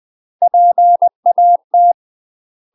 Morse code, heard as a keyed tone, about 20 wpm, 700 Hz. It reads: PAT
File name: UK MASTER QSOs TRAINING / 20WPM / UK MASTER QSOs 0012